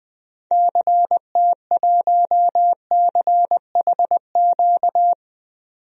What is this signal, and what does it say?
Morse code, heard as a keyed tone, about 20 wpm, 700 Hz. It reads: CT1CHQ
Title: DXCC-100 Call Signs - 6 Characters - 20wpm